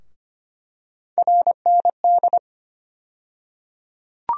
Morse code, heard as a keyed tone, about 25 wpm, 700 Hz. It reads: RND
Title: USA Call Sign Postfixes - 3 Letters 25wpm